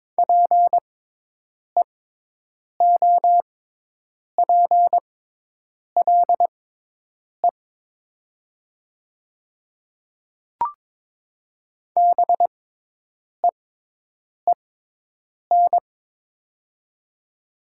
Morse code, heard as a keyed tone, about 22 wpm, spaced as for 8 wpm, 700 Hz. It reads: PEOPLE BEEN
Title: 6x Character Spacing - Top 200 Words - 22wpm